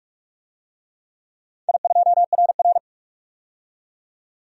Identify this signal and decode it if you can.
Morse code, heard as a keyed tone, about 44 wpm, 700 Hz. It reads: I2RR
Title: DXCC-100 Call Signs - 4 Characters - 45wpm